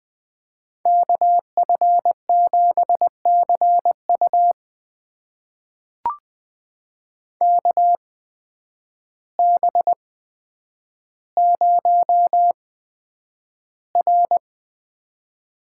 Morse code, heard as a keyed tone, about 20 wpm, spaced as for 9 wpm, 700 Hz. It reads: KF7CU KB0R
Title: USA Call Signs - 5 Characters - 8x Character Spacing - 20wpm